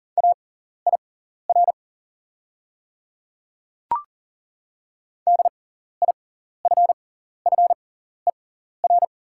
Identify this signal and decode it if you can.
Morse code, heard as a keyed tone, about 40 wpm, spaced as for 14 wpm, 690 Hz. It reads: AIR DIFFER
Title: 6x Character Spacing - Top 200 Words - 40wpm